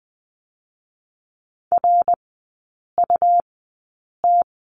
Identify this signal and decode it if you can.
Morse code, heard as a keyed tone, about 20 wpm, spaced as for 8 wpm, 700 Hz. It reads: RUT
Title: 127 Lesson 21e - Number 6 - Three Characters 20wpm